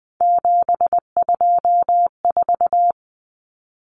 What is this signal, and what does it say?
Morse code, heard as a keyed tone, about 20 wpm, 700 Hz. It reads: 724
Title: Numbers - 3-Digit 20wpm